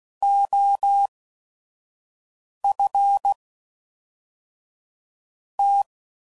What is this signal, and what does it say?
Morse code, heard as a keyed tone, about 16 wpm, spaced as for 5 wpm, 800 Hz. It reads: OF T
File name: CW Code Groups @5WPM / Technical Subjects / AND Gate